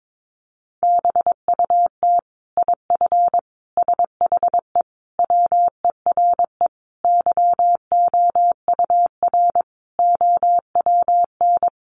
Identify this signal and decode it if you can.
Morse code, heard as a keyed tone, about 22 wpm, 700 Hz. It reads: BUT IF SHE WERE YOUR OWN
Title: Sentences from Top 1500 Words 22wpm